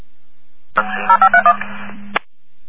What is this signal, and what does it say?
DTMF, heard as telephone touch tones, tones of about 80 ms. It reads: 0531
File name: Sylvan Beach 17 07 51